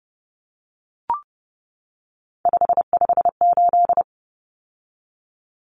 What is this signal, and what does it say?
Morse code, heard as a keyed tone, about 30 wpm, 700 Hz. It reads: 558
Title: Common Cut Numbers - 3-Digit 30wpm